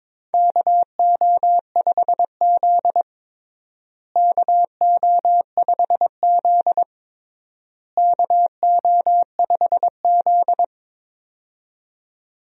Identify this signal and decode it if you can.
Morse code, heard as a keyed tone, about 22 wpm, 700 Hz. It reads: KO5Z KO5Z KO5Z
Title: USA Call Signs - 4 Characters - Repeated 3x - 22wpm